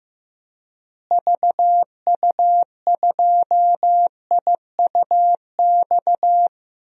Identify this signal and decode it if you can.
Morse code, heard as a keyed tone, about 15 wpm, 700 Hz. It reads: VU2IUX
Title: DXCC-100 Call Signs - 6 Characters - Repeated 2x - 15wpm